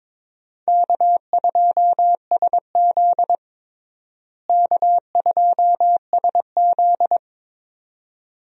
Morse code, heard as a keyed tone, about 22 wpm, 700 Hz. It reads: K2SZ K2SZ
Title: USA Call Signs - 4 Characters - Repeated 2x - 22wpm